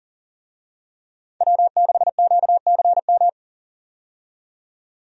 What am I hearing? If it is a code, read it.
Morse code, heard as a keyed tone, about 38 wpm, 700 Hz. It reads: W6QCM
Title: USA Call Signs - 5 Characters - 40wpm